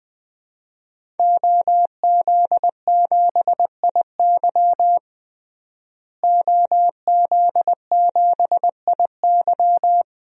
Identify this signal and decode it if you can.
Morse code, heard as a keyed tone, about 20 wpm, 700 Hz. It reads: OZ7IY OZ7IY
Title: DXCC-100 Call Signs - 5 Characters - Repeated 3x - 20wpm